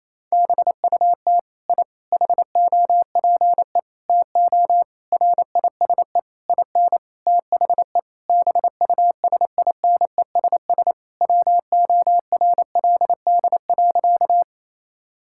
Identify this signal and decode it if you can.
Morse code, heard as a keyed tone, about 28 wpm, 700 Hz. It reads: BUT I HOPE TO RISE IN THE BUSINESS WORLD.
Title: Sentences from Top 1000 Words - Review Part 6 - 28wpm